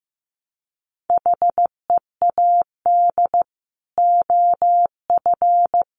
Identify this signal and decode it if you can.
Morse code, heard as a keyed tone, about 15 wpm, 700 Hz. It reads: HEAD OF
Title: Sets of 3 Words 15wpm